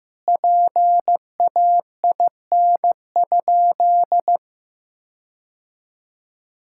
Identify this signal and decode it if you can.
Morse code, heard as a keyed tone, about 15 wpm, 700 Hz. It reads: PAIN?
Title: Sentences from Top 1600 Words 15wpm